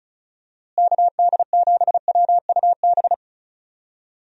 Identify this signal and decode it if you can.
Morse code, heard as a keyed tone, about 36 wpm, 700 Hz. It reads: KD7WUB
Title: USA Call Signs - 6 Characters - Repeated 3x - 35wpm